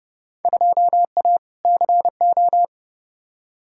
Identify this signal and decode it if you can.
Morse code, heard as a keyed tone, about 30 wpm, 700 Hz. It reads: 2A CO
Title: Field Day Exchanges - 30wpm